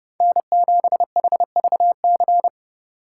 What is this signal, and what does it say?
Morse code, heard as a keyed tone, about 30 wpm, 700 Hz. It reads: N7HVC
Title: USA Call Signs - 2x Character Spacing - 30wpm